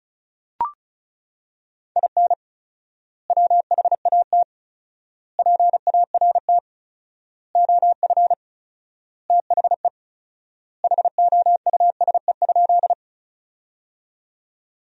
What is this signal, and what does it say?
Morse code, heard as a keyed tone, about 36 wpm, 700 Hz. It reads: IN WHAT PART OF THE HOUSE?
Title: 4x Word-Spacing - Sentences from Top 200 Words - 35wpm